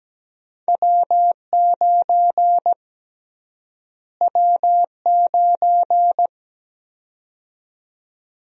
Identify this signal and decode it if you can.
Morse code, heard as a keyed tone, about 17 wpm, 700 Hz. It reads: W9 W9
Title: USA Call Sign Prefixes - Repeated 2x - 17wpm